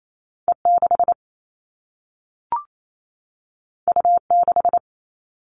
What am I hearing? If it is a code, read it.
Morse code, heard as a keyed tone, about 28 wpm, 700 Hz. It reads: E6 U6
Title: All Cut Numbers - 2-Digit 28wpm